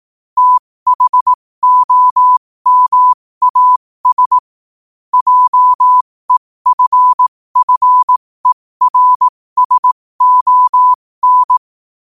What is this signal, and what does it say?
Morse code, heard as a keyed tone, about 18 wpm, spaced as for 15 wpm, 1000 Hz. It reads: THOMAS JEFFERSON
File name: qotdfiles / 15wpm